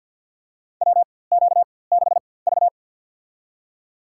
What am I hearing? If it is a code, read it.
Morse code, heard as a keyed tone, about 48 wpm, spaced as for 23 wpm, 690 Hz. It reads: WQ6V
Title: USA Call Signs - 4x Character Spacing - 50wpm